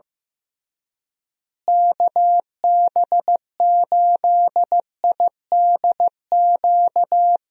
Morse code, heard as a keyed tone, about 15 wpm, 700 Hz. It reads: KB8IDQ
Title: USA Call Signs - 6 Characters - Repeated 3x - 15wpm